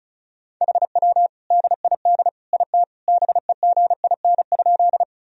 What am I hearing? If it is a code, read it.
Morse code, heard as a keyed tone, about 34 wpm, 700 Hz. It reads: HW DID IT BEGIN?
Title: Sentences from Top 300 Words and Abbreviations 35wpm